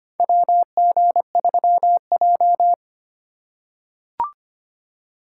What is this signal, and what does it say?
Morse code, heard as a keyed tone, about 25 wpm, 700 Hz. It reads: WG3J
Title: USA Call Signs - 4 Characters - 3x Character Spacing - 25wpm